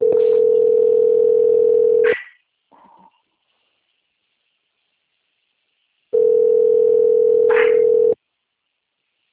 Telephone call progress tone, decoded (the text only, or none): ringback tone